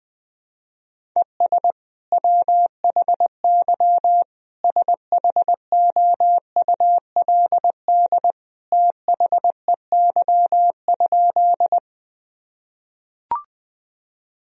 Morse code, heard as a keyed tone, about 21 wpm, 700 Hz. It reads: ES WHY SHOULD THEY?
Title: Sentences from Top 300 Words and Abbreviations 20wpm